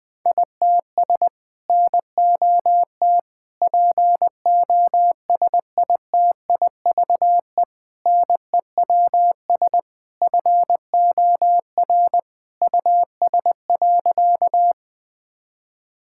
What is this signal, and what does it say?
Morse code, heard as a keyed tone, about 20 wpm, 700 Hz. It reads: ITS NOT POSITIVE NEWS FOR US.
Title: Sentences from Top 1000 Words - Review Part 5 - 20wpm